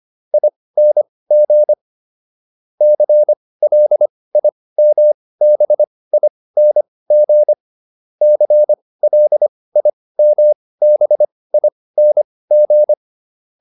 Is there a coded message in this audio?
Yes: Morse code from 0.3 to 12.9 s, heard as a keyed tone, about 25 wpm, spaced as for 18 wpm, 600 Hz. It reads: ING CLIMBING CLIMBING